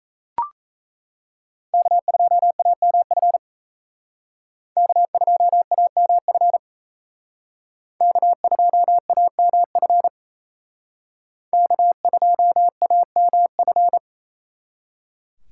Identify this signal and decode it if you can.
Morse code, heard as a keyed tone, about 36 wpm, 700 Hz. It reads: K2AMF K2AMF K2AMF K2AMF